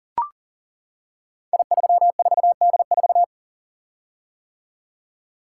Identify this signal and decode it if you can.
Morse code, heard as a keyed tone, about 39 wpm, 700 Hz. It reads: I34D4
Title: Code Groups - 5 Letter:Numbers - 40wpm